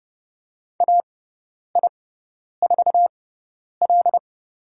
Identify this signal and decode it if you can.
Morse code, heard as a keyed tone, about 30 wpm, spaced as for 10 wpm, 700 Hz. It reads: AI4L
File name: Lesson 16g - Letter c - Call Signs 30wpm